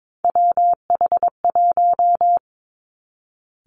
Morse code, heard as a keyed tone, about 22 wpm, 700 Hz. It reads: WH1
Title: Three Challenging Letter-Number Combinations 22wpm